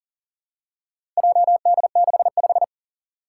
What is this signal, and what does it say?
Morse code, heard as a keyed tone, about 38 wpm, 700 Hz. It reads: JD65
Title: Four Challenging Letter-Number Combinations 40wpm